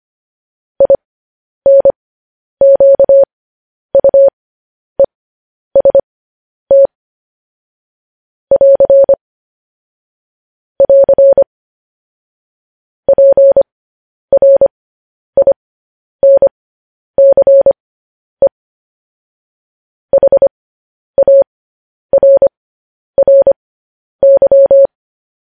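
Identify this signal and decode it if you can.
Morse code, heard as a keyed tone, about 25 wpm, spaced as for 10 wpm, 550 Hz. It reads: INQUEST <AR> <AR> PRINCE HARRY
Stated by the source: Morse code news headlines at 10.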